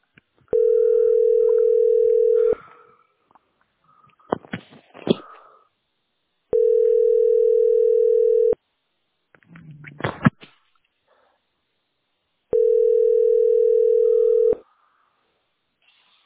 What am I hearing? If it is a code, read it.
Ringback tone (North America), 440 + 480 Hz.